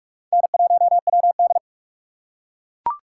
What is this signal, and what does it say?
Morse code, heard as a keyed tone, about 42 wpm, 700 Hz. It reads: N1WD
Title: USA Call Signs - 4 Characters - 3x Character Spacing - 45wpm